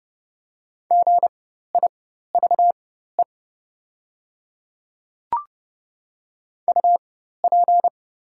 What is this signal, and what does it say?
Morse code, heard as a keyed tone, about 30 wpm, spaced as for 14 wpm, 700 Hz. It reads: GIVE UP
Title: 4x Character Spacing - Top 200 Words - 30wpm